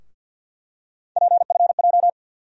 Morse code, heard as a keyed tone, about 47 wpm, 700 Hz. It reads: PFJ
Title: USA Call Sign Postfixes - 3 Letters 50wpm